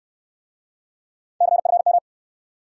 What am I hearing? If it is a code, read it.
Morse code, heard as a keyed tone, about 70 wpm, 700 Hz. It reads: CFM